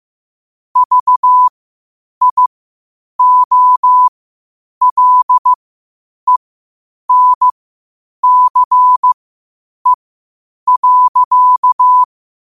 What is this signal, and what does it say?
Morse code, heard as a keyed tone, about 15 wpm, spaced as for 8 wpm, 1000 Hz. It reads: VIOLENCE.
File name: qotdfiles / 8wpm